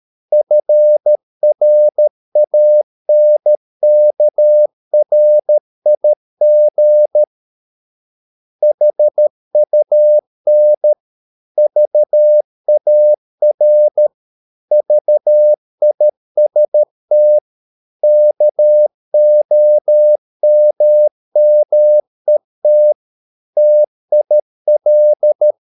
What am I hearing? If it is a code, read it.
Morse code, heard as a keyed tone, about 13 wpm, 600 Hz. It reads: FRANKRIG HUN VAR VIST KOMMET TIL
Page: Ferie 13 wpm